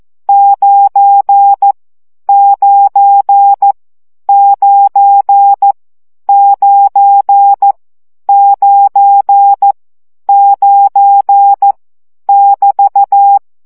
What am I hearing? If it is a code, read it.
Morse code, heard as a keyed tone, about 14 wpm, 800 Hz. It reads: 9 9 9 9 9 9 <BT>
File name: CW gyakorlas 9 12 baud